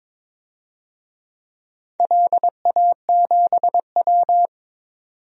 Morse code, heard as a keyed tone, about 22 wpm, 700 Hz. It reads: LA7W
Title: DXCC-100 Call Signs - 4 Characters - 22wpm